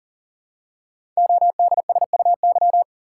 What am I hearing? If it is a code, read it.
Morse code, heard as a keyed tone, about 41 wpm, 700 Hz. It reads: ODSUY
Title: Code Groups - 5 Letters - 40wpm